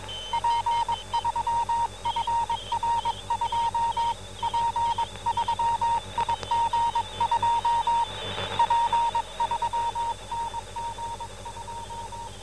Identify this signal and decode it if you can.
Morse code, heard as a keyed tone, about 22 wpm, 930 Hz. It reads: P3FR2 P3IG2 P3NG2